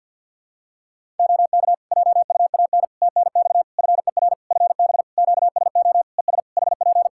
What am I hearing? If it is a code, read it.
Morse code, heard as a keyed tone, about 51 wpm, 700 Hz. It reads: OK JUAN TNX FER FB QSO ES HP
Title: K1USN SST QSO Practice 50wpm